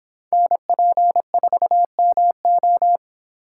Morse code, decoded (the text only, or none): NP4MO